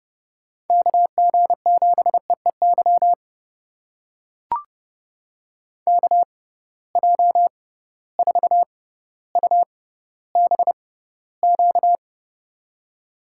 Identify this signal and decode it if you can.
Morse code, heard as a keyed tone, about 31 wpm, spaced as for 15 wpm, 700 Hz. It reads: KG7EEY KJ4UBQ
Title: USA Call Signs - 6 Characters - 6x Character Spacing - 30wpm